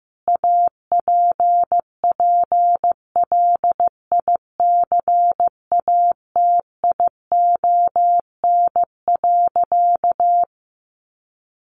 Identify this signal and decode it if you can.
Morse code, heard as a keyed tone, about 15 wpm, 700 Hz. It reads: APPLICATION.
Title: Sentences from Top 1200 Words 15wpm